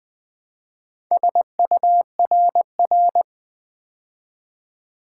Morse code, heard as a keyed tone, about 20 wpm, 700 Hz. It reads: SURR
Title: Build A Word - 2 Letters At A Time - 8-Letter Words - 20wpm